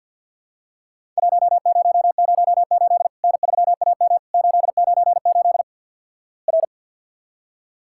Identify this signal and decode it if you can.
Morse code, heard as a keyed tone, about 52 wpm, 700 Hz. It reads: 1008 N3AM 898 R
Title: Worked All Europe DX Contest - QTC Exchange with Repeats - 50wpm